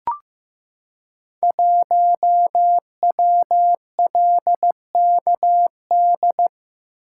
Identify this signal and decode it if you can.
Morse code, heard as a keyed tone, about 15 wpm, 700 Hz. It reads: 1WLKD
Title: Code Groups - 5 Letter:Numbers - 15wpm